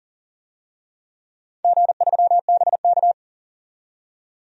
Morse code, heard as a keyed tone, about 40 wpm, 700 Hz. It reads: G3BK
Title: DXCC-100 Call Signs - 4 Characters - 40wpm